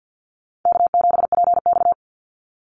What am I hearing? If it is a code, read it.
Morse code, heard as a keyed tone, about 50 wpm, 700 Hz. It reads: K7PX
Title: NA SOTA Chaser Callsigns 50wpm